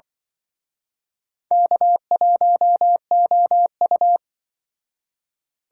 Morse code, heard as a keyed tone, about 24 wpm, 700 Hz. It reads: K1OU
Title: Speed Racing - USA 1x2 Call Signs 20wpm